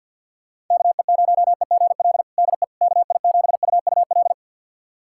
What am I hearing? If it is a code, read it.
Morse code, heard as a keyed tone, about 52 wpm, 700 Hz. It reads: KE0EGL DE KI7UUL
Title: USA Callsign DE Callsign 50wpm